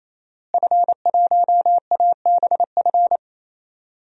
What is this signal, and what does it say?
Morse code, heard as a keyed tone, about 28 wpm, 700 Hz. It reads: F1ABF
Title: DXCC-100 Call Signs - 5 Characters - Repeated 3x - 28wpm